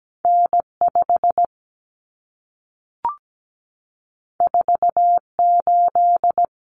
Morse code, heard as a keyed tone, about 17 wpm, 700 Hz. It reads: N5 48
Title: Common Cut Numbers - 2-Digit 17wpm